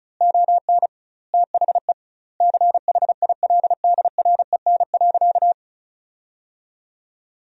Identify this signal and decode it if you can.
Morse code, heard as a keyed tone, about 34 wpm, 700 Hz. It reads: ON THE CHILDREN.
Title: 2x Word-Spacing - Sentences from Top 300 Words - 35wpm